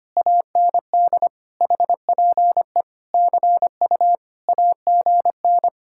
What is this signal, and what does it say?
Morse code, heard as a keyed tone, about 24 wpm, 700 Hz. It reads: AND HPE CU AGN
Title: Speed-Racing Common QSO Phrases - 25wpm